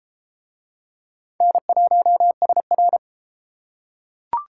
Morse code, heard as a keyed tone, about 28 wpm, 700 Hz. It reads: N1SR